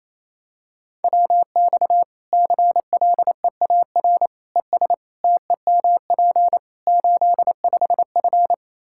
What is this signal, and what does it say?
Morse code, heard as a keyed tone, about 27 wpm, 700 Hz. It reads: WX CLEAR ES TEMP 85F
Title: Practice QSO - Exchange 2 - Info - 28wpm